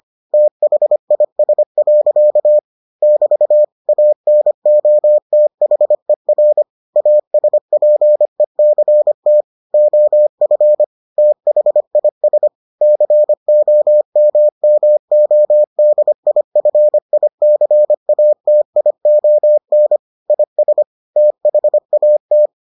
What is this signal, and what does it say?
Morse code, heard as a keyed tone, about 25 wpm, 600 Hz. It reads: THIS. <BT> ANOTHER ASPECT OF THIS COMMODIFICATION IS THAT